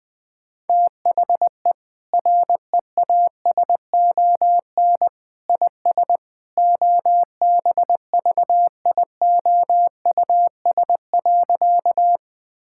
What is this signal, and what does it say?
Morse code, heard as a keyed tone, about 21 wpm, 700 Hz. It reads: THE REASON IS OBVIOUS.
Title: Sentences from Top 1600 Words 20wpm